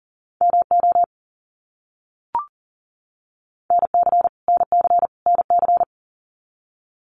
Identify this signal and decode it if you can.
Morse code, heard as a keyed tone, about 38 wpm, 700 Hz. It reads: MO NC NC NC
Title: US States Repeated 3x 40wpm